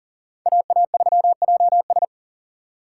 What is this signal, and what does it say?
Morse code, heard as a keyed tone, about 37 wpm, 700 Hz. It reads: AA3JS
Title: USA Call Signs - 6x Character Spacing - 40wpm